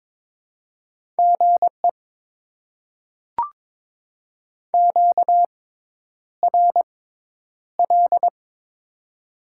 Morse code, heard as a keyed tone, about 23 wpm, spaced as for 10 wpm, 700 Hz. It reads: GE QRL
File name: Top 20 QSO Elements - 6x Character Spacing - 22wpm